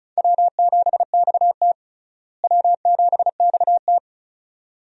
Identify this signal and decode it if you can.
Morse code, heard as a keyed tone, about 34 wpm, 700 Hz. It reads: W7XT W7XT
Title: USA Call Signs - 4 Characters - Repeated 2x - 35wpm